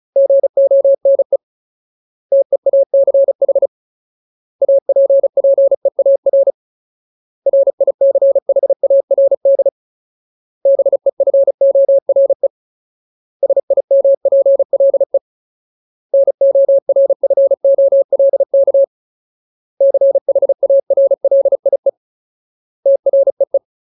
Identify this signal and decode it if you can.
Morse code, heard as a keyed tone, about 34 wpm, 550 Hz. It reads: GONE TEACH APPEAR RICHARD BEFORE SIMPLE NORFOLK CHARLIE TREE
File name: Ultimate Word List / 35WPM Ultimate